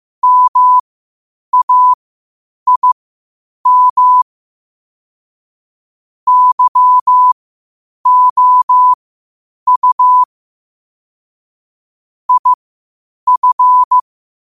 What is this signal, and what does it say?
Morse code, heard as a keyed tone, about 15 wpm, spaced as for 8 wpm, 1000 Hz. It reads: MAIM YOU IF
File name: qotdfiles / 8wpm